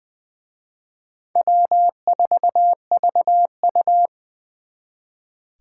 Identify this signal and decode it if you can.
Morse code, heard as a keyed tone, about 24 wpm, 700 Hz. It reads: W4VU
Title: Speed Racing - USA 1x2 Call Signs 20wpm